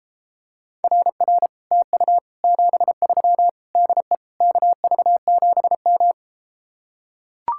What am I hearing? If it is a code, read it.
Morse code, heard as a keyed tone, about 33 wpm, 700 Hz. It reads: RR TU 73 DE KV7M